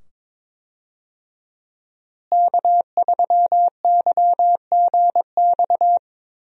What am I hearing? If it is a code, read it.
Morse code, heard as a keyed tone, about 22 wpm, 700 Hz. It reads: K3YGX
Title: USA Call Signs - 5 Characters - Repeated 2x - 22wpm